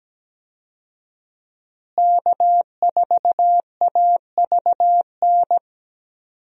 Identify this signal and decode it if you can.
Morse code, heard as a keyed tone, about 17 wpm, 700 Hz. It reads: K4AVN
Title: USA Call Signs - 5 Characters - Repeated 3x - 17wpm